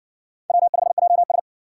Morse code, heard as a keyed tone, about 52 wpm, 700 Hz. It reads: W5JS